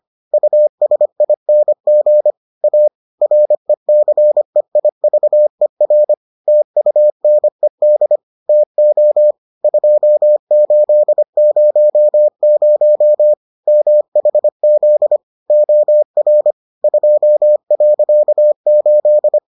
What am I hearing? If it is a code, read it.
Morse code, heard as a keyed tone, about 25 wpm, 600 Hz. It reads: USING A RECEIVER TUNED TO 2800 MHZ OR 2.8